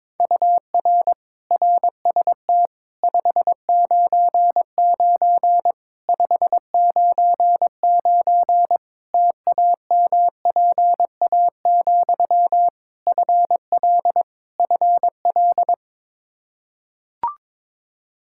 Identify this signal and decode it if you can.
Morse code, heard as a keyed tone, about 22 wpm, 700 Hz. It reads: UR RST 599 599 TAMPA, FL FL
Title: K1USN SST QSO Practice 22wpm